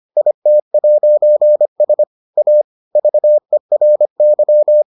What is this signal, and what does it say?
Morse code, heard as a keyed tone, about 25 wpm, 600 Hz. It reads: IT'S A VERY